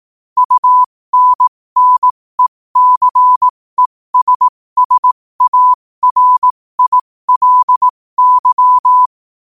Morse code, heard as a keyed tone, about 18 wpm, spaced as for 15 wpm, 1000 Hz. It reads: UNNECESSARILY